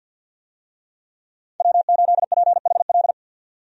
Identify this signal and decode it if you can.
Morse code, heard as a keyed tone, about 53 wpm, 700 Hz. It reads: W8PHL
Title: USA Call Signs - 5 Characters - 50wpm